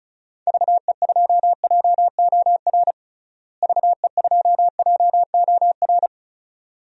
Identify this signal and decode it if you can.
Morse code, heard as a keyed tone, about 36 wpm, 700 Hz. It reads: VE2JOR VE2JOR
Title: DXCC-100 Call Signs - 6 Characters - Repeated 2x - 35wpm